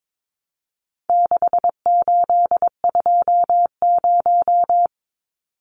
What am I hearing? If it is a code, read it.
Morse code, heard as a keyed tone, about 22 wpm, 700 Hz. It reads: 6820
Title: Numbers - 4-Digit 22wpm